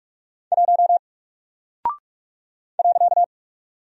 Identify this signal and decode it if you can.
Morse code, heard as a keyed tone, about 44 wpm, 700 Hz. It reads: 1 .
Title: Single Character (Letter:Number:Punctuation:Prosign) Alt 45wpm